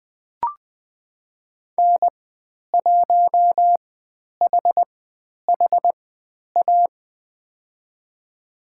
Morse code, heard as a keyed tone, about 20 wpm, spaced as for 10 wpm, 700 Hz. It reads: N1HHA
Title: 095 Lesson 16g - Letter c - Call Signs 20wpm